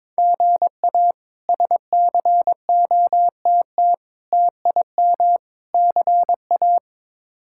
Morse code, heard as a keyed tone, about 22 wpm, 700 Hz. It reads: GA SCOTT TIM CA
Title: K1USN SST QSO Practice - 22wpm